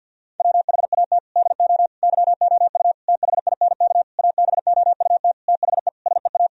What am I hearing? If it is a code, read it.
Morse code, heard as a keyed tone, about 50 wpm, 700 Hz. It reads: WHAT DO YOU THINK ABOUT THE SEA
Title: Sentences from Top 1000 Words - Review Part 4 - 50wpm